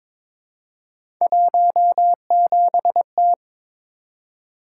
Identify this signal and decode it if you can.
Morse code, heard as a keyed tone, about 22 wpm, 700 Hz. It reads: 17T
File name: practice / 22wpm / Three Letter-Number Combinations - 22wpm